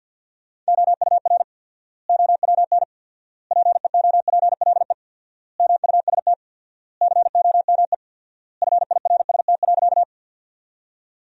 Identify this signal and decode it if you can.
Morse code, heard as a keyed tone, about 48 wpm, 700 Hz. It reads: OUR OWN PEOPLE MUST COME FIRST.
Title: 4x Word-Spacing - Sentences from Top 300 Words - 50wpm